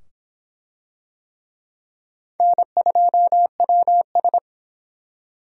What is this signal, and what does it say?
Morse code, heard as a keyed tone, about 22 wpm, 700 Hz. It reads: N2WS